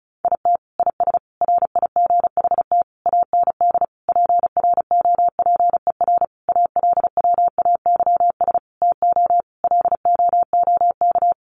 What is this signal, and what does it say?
Morse code, heard as a keyed tone, about 34 wpm, 700 Hz. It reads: IT IS RIGHT AND PROPER ALWAYS TO LOOK
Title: Sentences from Top 1500 Words 35wpm